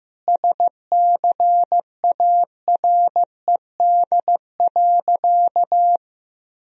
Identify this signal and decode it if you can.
Morse code, heard as a keyed tone, about 15 wpm, 700 Hz. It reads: SCARED.
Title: Sentences from Top 1000 Words - Review Part 4 - 15wpm